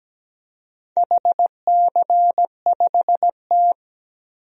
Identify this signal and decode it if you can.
Morse code, heard as a keyed tone, about 17 wpm, 700 Hz. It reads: HC5T
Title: DXCC-100 Call Signs - 4 Characters - 17wpm